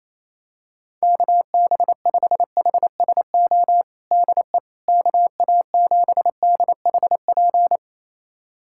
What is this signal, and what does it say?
Morse code, heard as a keyed tone, about 28 wpm, 700 Hz. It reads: KB5HSO DE KA7DHP
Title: USA Callsign DE Callsign 28wpm